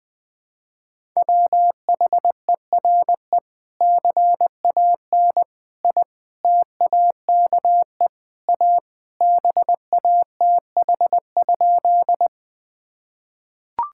Morse code, heard as a keyed tone, about 20 wpm, 700 Hz. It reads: WHERE CAN I TAKE A BATH?
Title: Sentences from Top 1600 Words 20wpm